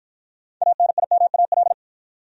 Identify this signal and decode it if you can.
Morse code, heard as a keyed tone, about 48 wpm, 700 Hz. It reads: ANIMAL